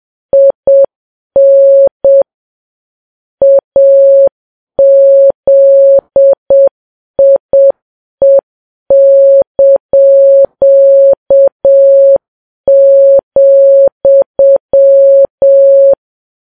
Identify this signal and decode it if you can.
Morse code, heard as a keyed tone, about 7 wpm, 550 Hz. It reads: IN AZIE),
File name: Files Web tls / Morse on9fv / morse7kps